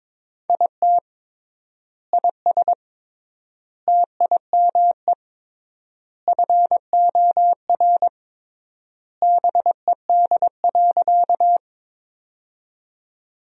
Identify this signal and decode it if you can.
Morse code, heard as a keyed tone, about 22 wpm, 700 Hz. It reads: IT IS TIME FOR BED.
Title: 3x Word-Spacing - Sentences from Top 500 Words - 22wpm